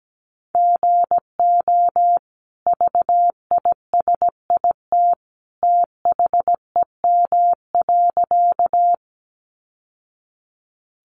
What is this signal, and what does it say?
Morse code, heard as a keyed tone, about 17 wpm, 700 Hz. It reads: GO VISIT THEM.
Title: Sentences from Top 900 Words 17wpm